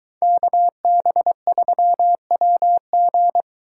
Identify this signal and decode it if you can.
Morse code, heard as a keyed tone, about 23 wpm, 700 Hz. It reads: KB3WG